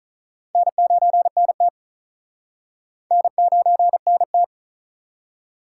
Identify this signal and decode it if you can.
Morse code, heard as a keyed tone, about 48 wpm, 700 Hz. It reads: N9NT N9NT